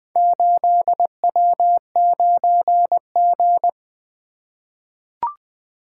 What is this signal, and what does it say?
Morse code, heard as a keyed tone, about 20 wpm, 700 Hz. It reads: 8W9G
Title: Four Letter-Number Combinations - 20wpm